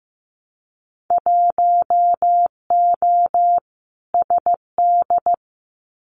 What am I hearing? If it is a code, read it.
Morse code, heard as a keyed tone, about 15 wpm, 700 Hz. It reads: 1O SD
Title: Winter Field Day Exchanges Encore 15wpm